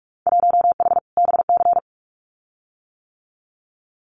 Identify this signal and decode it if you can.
Morse code, heard as a keyed tone, about 46 wpm, 700 Hz. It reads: 1H BC
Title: Winter Field Day Exchanges 45wpm